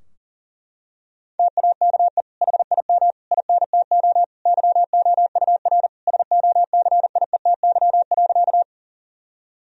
Morse code, heard as a keyed tone, about 40 wpm, 700 Hz. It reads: TAKE HIM INTO YOUR SOCIETY.
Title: Sentences from Top 1000 Words - Review Part 4 - 40wpm